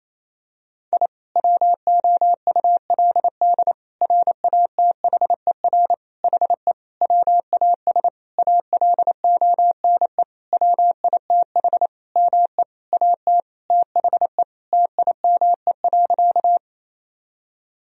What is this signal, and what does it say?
Morse code, heard as a keyed tone, about 28 wpm, 700 Hz. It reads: I WOULD RATHER HE WAS ALONE WITH ME AT THE TIME.
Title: Sentences from Top 1000 Words - Review Part 4 - 28wpm